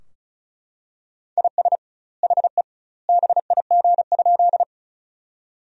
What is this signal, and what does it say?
Morse code, heard as a keyed tone, about 36 wpm, 700 Hz. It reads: IS HE BIG?
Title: 2x Word-Spacing - Sentences from Top 200 Words - 35wpm